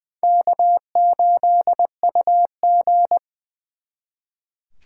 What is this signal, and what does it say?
Morse code, heard as a keyed tone, about 23 wpm, 700 Hz. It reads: K8UG
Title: Speed Racing - USA 1x2 Call Signs 20wpm